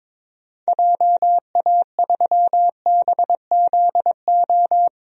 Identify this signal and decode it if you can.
Morse code, heard as a keyed tone, about 21 wpm, 700 Hz. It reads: JA3BZO
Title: DXCC-100 Call Signs - 6 Characters - 22wpm